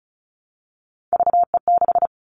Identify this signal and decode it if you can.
Morse code, heard as a keyed tone, about 36 wpm, 700 Hz. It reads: VE6
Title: Callsign Prefixes for Top 50 DXCC - 35wpm